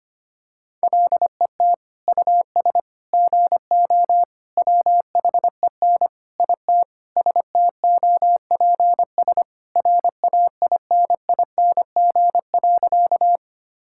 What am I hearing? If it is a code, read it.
Morse code, heard as a keyed tone, about 25 wpm, 700 Hz. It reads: LET US GO WHEN IT STOPS RAINING.
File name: practice / 25wpm / Sentences with 2-Word Phrases - 25wpm